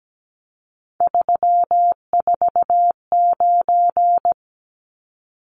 Morse code, heard as a keyed tone, about 17 wpm, 700 Hz. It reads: 349
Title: Numbers - 3-Digit 17wpm